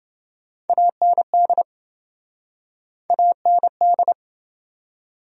Morse code, heard as a keyed tone, about 26 wpm, 700 Hz. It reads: AND AND